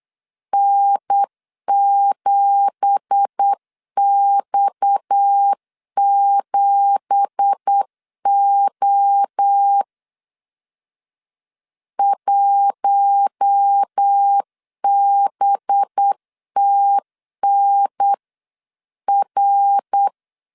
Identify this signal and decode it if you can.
Morse code, heard as a keyed tone, about 8 wpm, 800 Hz. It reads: N7X7O 1BTNR
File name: CW / DET / 1 9 1500C S B 8wpm 5624